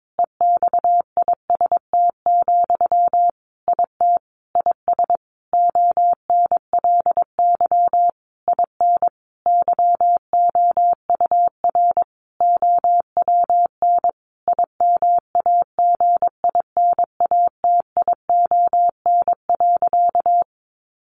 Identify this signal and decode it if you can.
Morse code, heard as a keyed tone, about 22 wpm, 700 Hz. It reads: EXIST, IT IS ONLY IN YOUR OWN IMAGINATION.
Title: Sentences from Top 1300 Words 22wpm